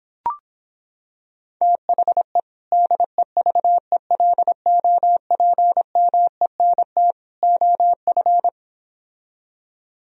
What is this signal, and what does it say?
Morse code, heard as a keyed tone, about 26 wpm, 700 Hz. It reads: THE DEVELOPMENT OF